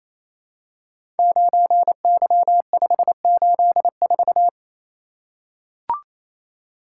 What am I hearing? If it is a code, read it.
Morse code, heard as a keyed tone, about 28 wpm, 700 Hz. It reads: 9Y584
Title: Code Groups - 5 Letter:Numbers - 28wpm